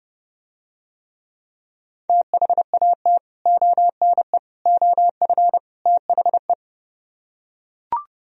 Morse code, heard as a keyed tone, about 30 wpm, 700 Hz. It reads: THAT ONE OF THE